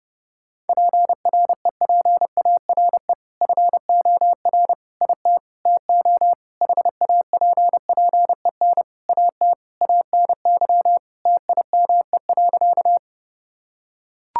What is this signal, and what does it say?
Morse code, heard as a keyed tone, about 28 wpm, 700 Hz. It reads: PREPARE FOR IT TO HAPPEN AT ANY TIME.
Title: Sentences from Top 1000 Words - Review Part 6 - 30wpm